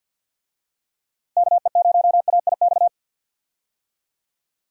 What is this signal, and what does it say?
Morse code, heard as a keyed tone, about 52 wpm, 700 Hz. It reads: KE0AIX
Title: USA Call Signs - 6 Characters - Repeated 2x - 50wpm